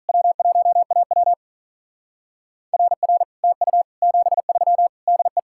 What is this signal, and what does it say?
Morse code, heard as a keyed tone, about 45 wpm, 700 Hz. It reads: W1AW RR TU 73 DE